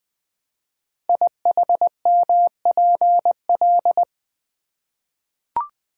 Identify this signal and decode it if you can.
Morse code, heard as a keyed tone, about 20 wpm, 700 Hz. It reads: IHMPL
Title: Code Groups - 5 Letters - 20wpm